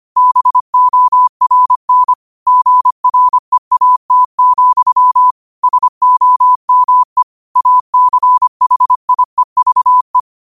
Morse code, heard as a keyed tone, about 25 wpm, 1000 Hz. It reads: BORN GREAT, SOME ACHIEVE
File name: qotdfiles / 25wpm